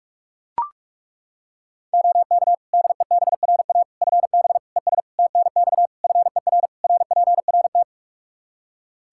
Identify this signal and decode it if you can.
Morse code, heard as a keyed tone, about 41 wpm, 700 Hz. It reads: OK DEBRA FB ES TNX FER RPRT
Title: Practice QSO - Exchange 2 - Info - 45wpm